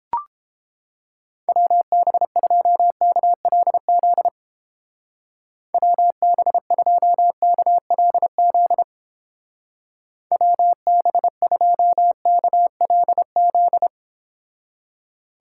Morse code, heard as a keyed tone, about 29 wpm, 700 Hz. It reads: WB2KLZ WB2KLZ WB2KLZ